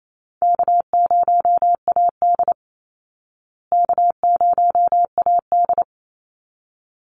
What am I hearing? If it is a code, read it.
Morse code, heard as a keyed tone, about 28 wpm, 700 Hz. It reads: K0AD K0AD
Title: Call Signs Repeated 3x - 28wpm